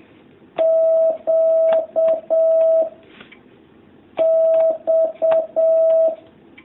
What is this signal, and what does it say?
Morse code, heard as a keyed tone, about 7 wpm, 650 Hz. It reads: Q X
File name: jh8cbh / 2019cw